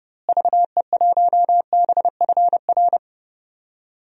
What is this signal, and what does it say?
Morse code, heard as a keyed tone, about 29 wpm, 700 Hz. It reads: VE1BFR
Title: DXCC-100 Call Signs - 6 Characters - Repeated 2x - 30wpm